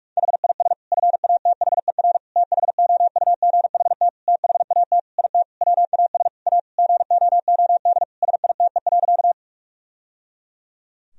Morse code, heard as a keyed tone, about 46 wpm, 700 Hz. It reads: HIS FATHER THOUGHT THAT IT WAS A GOOD SITE.
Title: Sentences from Top 1000 Words - Review Part 4 - 45wpm